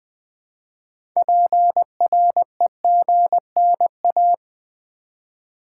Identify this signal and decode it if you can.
Morse code, heard as a keyed tone, about 20 wpm, 700 Hz. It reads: PREGNA
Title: Build A Word - 2 Letters At A Time - 8-Letter Words - 20wpm